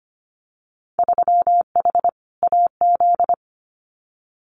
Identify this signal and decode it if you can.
Morse code, heard as a keyed tone, about 25 wpm, 700 Hz. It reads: 3H AZ
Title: Winter Field Day Exchanges Encore 25wpm